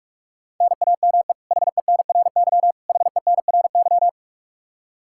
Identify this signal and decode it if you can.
Morse code, heard as a keyed tone, about 44 wpm, 700 Hz. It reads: NAME HENRY HENRY
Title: Practice QSO - Exchange 1 - Beginning - 45wpm